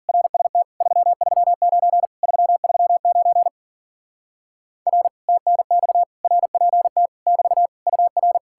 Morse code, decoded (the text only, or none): RST 339 339 R TNX RPT <BT> UR